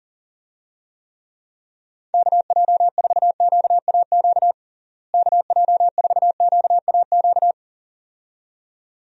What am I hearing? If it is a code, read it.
Morse code, heard as a keyed tone, about 37 wpm, 700 Hz. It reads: KJ4QAQ KJ4QAQ
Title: USA Call Signs - 6 Characters - Repeated 2x - 40wpm